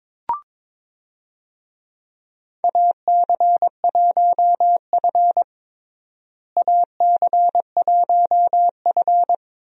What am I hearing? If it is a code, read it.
Morse code, heard as a keyed tone, about 22 wpm, 700 Hz. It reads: AC1F AC1F
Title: USA Call Signs - 4 Characters - Repeated 2x - 22wpm